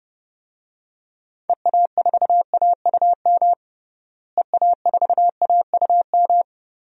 Morse code, heard as a keyed tone, about 30 wpm, 700 Hz. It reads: EA4AUM EA4AUM
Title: DXCC-100 Call Signs - 6 Characters - Repeated 3x - 30wpm